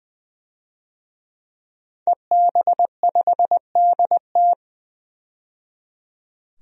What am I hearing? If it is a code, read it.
Morse code, heard as a keyed tone, about 20 wpm, 700 Hz. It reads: EB5DT
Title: DXCC-100 Call Signs - 5 Characters - 20wpm